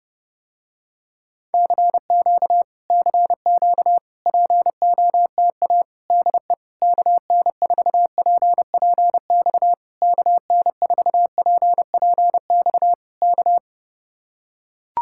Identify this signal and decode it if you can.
Morse code, heard as a keyed tone, about 30 wpm, 700 Hz. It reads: CQ CQ POTA DE KN4PPX KN4PPX K